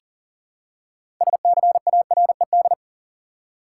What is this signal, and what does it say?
Morse code, heard as a keyed tone, about 40 wpm, 700 Hz. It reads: SCARED
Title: Six Letter Words - 40wpm